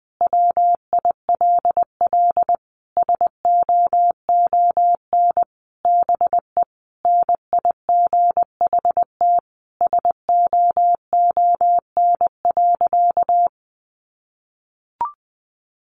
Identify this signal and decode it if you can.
Morse code, heard as a keyed tone, about 20 wpm, 700 Hz. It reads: WILL SOON BE NIGHT SOON.
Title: Sentences from Top 1300 Words 20wpm